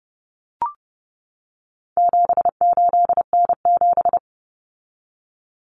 Morse code, heard as a keyed tone, about 30 wpm, 700 Hz. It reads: 78N7
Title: Common Cut Numbers - 4-Digit 30wpm